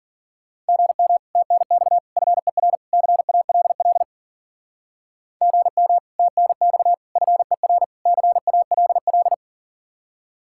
Morse code, heard as a keyed tone, about 50 wpm, 700 Hz. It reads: GM TNX FER CALL GM TNX FER CALL
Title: Speed-Racing Common QSO Phrases Encore - 40wpm